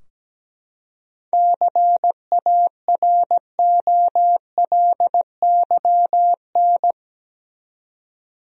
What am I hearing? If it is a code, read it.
Morse code, heard as a keyed tone, about 17 wpm, 700 Hz. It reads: CAROLYN
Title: USA Common Names - 17wpm